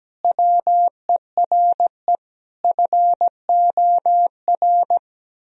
Morse code, heard as a keyed tone, about 17 wpm, 700 Hz. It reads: WERE FOR
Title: Sentences from Top 1000 Words - Review Part 2 - 17wpm